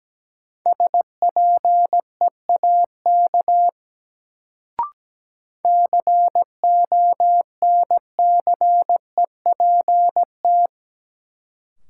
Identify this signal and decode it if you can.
Morse code, heard as a keyed tone, about 17 wpm, 700 Hz. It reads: SPEAK CONCEPT
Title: Sentences from Top 1100 Words - New Word Drill Alt - 17wpm